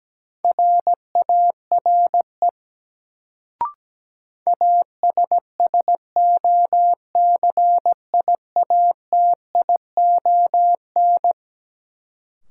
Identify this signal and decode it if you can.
Morse code, heard as a keyed tone, about 17 wpm, 700 Hz. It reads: RARE ASSOCIATION
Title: Sentences from Top 1100 Words - New Word Drill - 17wpm